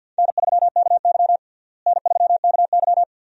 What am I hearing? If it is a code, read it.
Morse code, heard as a keyed tone, about 46 wpm, 700 Hz. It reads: N3KY N3KY
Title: USA Call Signs - 4 Characters - Repeated 2x - 50wpm